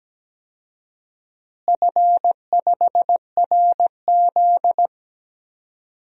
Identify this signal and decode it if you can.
Morse code, heard as a keyed tone, about 17 wpm, 700 Hz. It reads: F5RZ
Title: DXCC-100 Call Signs - 4 Characters - Repeated 3x - 17wpm